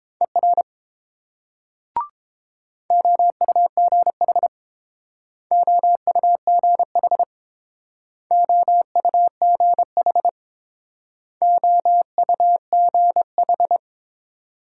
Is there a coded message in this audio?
Yes: Morse code from 0.2 to 13.8 s, heard as a keyed tone, about 30 wpm, 700 Hz. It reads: ER OUGH OUGH OUGH OUGH